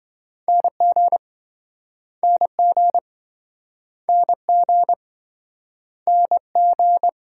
Morse code, heard as a keyed tone, about 28 wpm, 700 Hz. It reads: NG NG NG NG